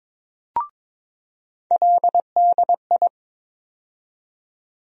Morse code, heard as a keyed tone, about 23 wpm, 700 Hz. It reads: LDI
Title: USA Call Sign Postfixes - 3 Letters 22wpm